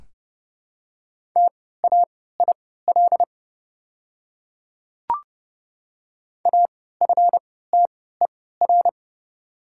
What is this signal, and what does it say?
Morse code, heard as a keyed tone, about 30 wpm, spaced as for 17 wpm, 700 Hz. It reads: TAIL AFTER
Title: 3x Character Spacing - Top 500 Words - 30wpm